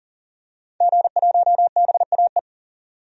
Morse code, heard as a keyed tone, about 40 wpm, 700 Hz. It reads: G1BAE
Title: DXCC-100 Call Signs - 5 Characters - 40wpm